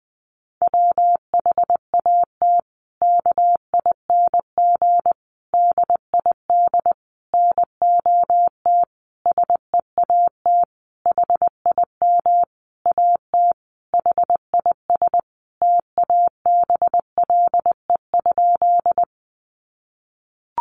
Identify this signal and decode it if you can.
Morse code, heard as a keyed tone, about 20 wpm, 700 Hz. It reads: WHAT KING DID NOT SEAT HIM AT HIS TABLE?
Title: Sentences from Top 1500 Words 20wpm